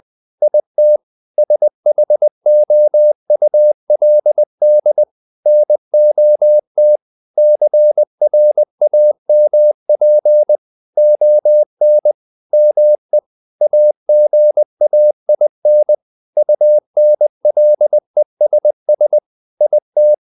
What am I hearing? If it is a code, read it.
Morse code, heard as a keyed tone, about 20 wpm, 600 Hz. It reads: IT SHOULD NOT CRAMP ON ME AGAIN UNLESS IT
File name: the old man and the sea part 2 20wpm